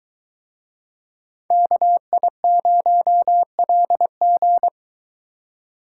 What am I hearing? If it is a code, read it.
Morse code, heard as a keyed tone, about 22 wpm, 700 Hz. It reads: KI0LG